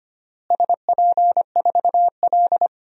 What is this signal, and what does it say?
Morse code, heard as a keyed tone, about 25 wpm, 700 Hz. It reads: SP4L
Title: DXCC-100 Call Signs - 4 Characters - Repeated 3x - 25wpm